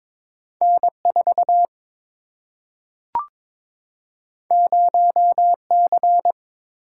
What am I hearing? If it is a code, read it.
Morse code, heard as a keyed tone, about 22 wpm, 700 Hz. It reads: N4 0C
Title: Single-Letter Single-Number 22wpm